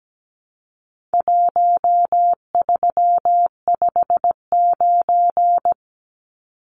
Morse code, heard as a keyed tone, about 17 wpm, 700 Hz. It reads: 1359
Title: Numbers - Contest Style - 4-Digit 17wpm